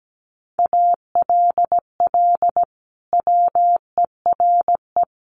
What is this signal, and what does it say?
Morse code, heard as a keyed tone, about 17 wpm, 700 Hz. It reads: ALL WERE
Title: Sentences from Top 1400 Words 17wpm